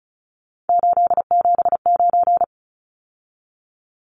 Morse code, heard as a keyed tone, about 34 wpm, 700 Hz. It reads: 879
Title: Numbers - 3-Digit 35wpm